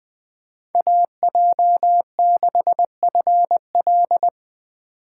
Morse code, heard as a keyed tone, about 20 wpm, 700 Hz. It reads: AJ6FL